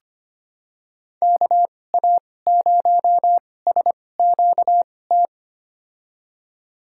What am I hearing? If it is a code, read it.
Morse code, heard as a keyed tone, about 25 wpm, spaced as for 18 wpm, 700 Hz. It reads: KA0SQT
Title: USA Call Signs - 6 Characters - 2x Character Spacing - 25wpm